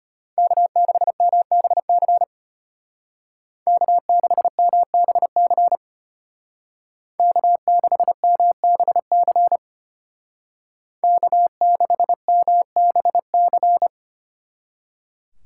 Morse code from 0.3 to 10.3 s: K6MBC K6MBC K6MBC

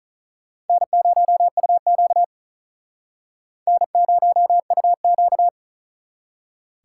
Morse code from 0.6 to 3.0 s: N0UQ

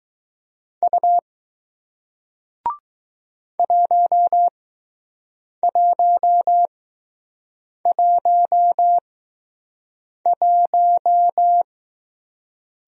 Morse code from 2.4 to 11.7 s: 1111

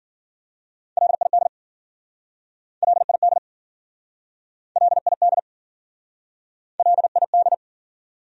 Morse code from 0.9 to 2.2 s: LID

